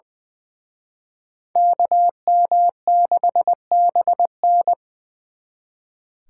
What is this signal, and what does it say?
Morse code, heard as a keyed tone, about 23 wpm, 700 Hz. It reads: KM6BN
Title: Speed Racing - USA 2x2 Call Signs 20wpm